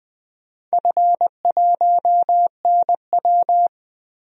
Morse code, heard as a keyed tone, about 21 wpm, 700 Hz. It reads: F1NW
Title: DXCC-100 Call Signs - 4 Characters - Repeated 2x - 20wpm